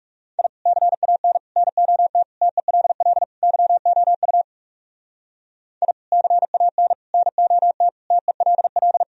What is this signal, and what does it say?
Morse code, heard as a keyed tone, about 37 wpm, 700 Hz. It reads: I CAN NOT TELL YOU I CAN NOT TELL
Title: Speed-Racing - Sets of 5 Words 30wpm